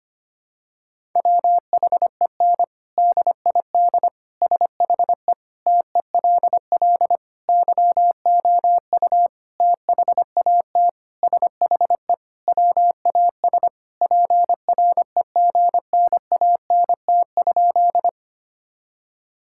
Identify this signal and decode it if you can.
Morse code, heard as a keyed tone, about 25 wpm, 700 Hz. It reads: WHEN DID SHE TELL YOU THAT SHE WAS PREGNANT?
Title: Sentences from Top 1000 Words - Review Part 6 - 25wpm